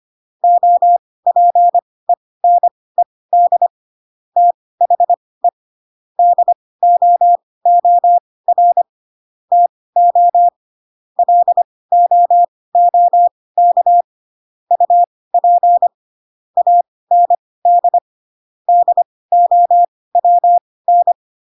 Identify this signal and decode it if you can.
Morse code, heard as a keyed tone, about 25 wpm, spaced as for 18 wpm, 700 Hz. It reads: OPENED THE DOOR TO LOOK UP AND DOWN
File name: Pinocchio / Pinocchio ch1 12 25 ik1hge